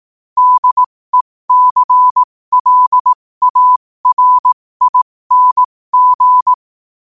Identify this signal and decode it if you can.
Morse code, heard as a keyed tone, about 18 wpm, spaced as for 15 wpm, 1000 Hz. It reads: DECLARING
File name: qotdfiles / 15wpm